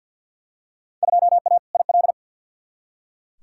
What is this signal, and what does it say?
Morse code, heard as a keyed tone, about 46 wpm, 700 Hz. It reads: 2A IL
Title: Field Day Exchanges - 50wpm